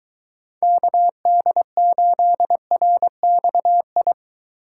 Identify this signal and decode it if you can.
Morse code, heard as a keyed tone, about 23 wpm, 700 Hz. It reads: KD8RXI